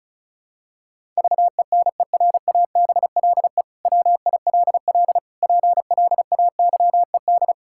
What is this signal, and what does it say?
Morse code, heard as a keyed tone, about 36 wpm, 700 Hz. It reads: VENERABLE WILL PLAYED
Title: Pangram Challenge v2 - 35wpm